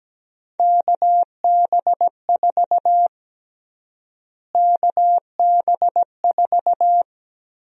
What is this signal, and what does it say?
Morse code, heard as a keyed tone, about 17 wpm, 700 Hz. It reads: KB4 KB4
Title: USA Call Sign Prefixes - Repeated 2x - 17wpm